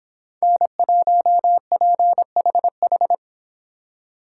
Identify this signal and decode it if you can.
Morse code, heard as a keyed tone, about 29 wpm, 700 Hz. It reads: N1PHH